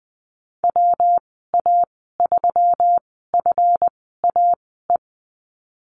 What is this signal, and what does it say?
Morse code, heard as a keyed tone, about 20 wpm, spaced as for 14 wpm, 700 Hz. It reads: WA3FAE
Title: 140 Lesson 23g - Letter f - Call Signs 20wpm